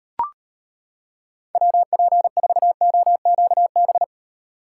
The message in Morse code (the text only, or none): WP4OQB